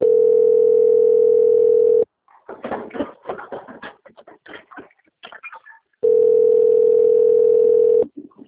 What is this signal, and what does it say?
Ringback tone (North America), 440 + 480 Hz.